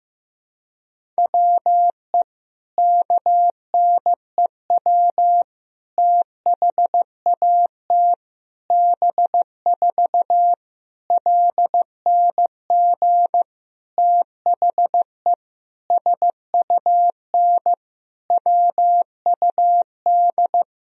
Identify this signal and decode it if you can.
Morse code, heard as a keyed tone, about 15 wpm, 700 Hz. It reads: WE KNEW THAT B4 LNG THE SUN WUD
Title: Sentences from Top 400 Words and Abbreviations 15wpm